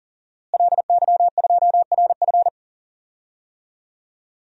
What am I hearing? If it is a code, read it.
Morse code, heard as a keyed tone, about 40 wpm, 700 Hz. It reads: LY2RF
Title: DXCC-100 Call Signs - 5 Characters - Repeated 2x - 40wpm